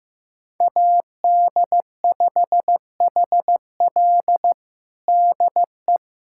Morse code, heard as a keyed tone, about 15 wpm, 700 Hz. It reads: AD5HL DE
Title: USA Callsign DE Callsign - 5 Characters - 15wpm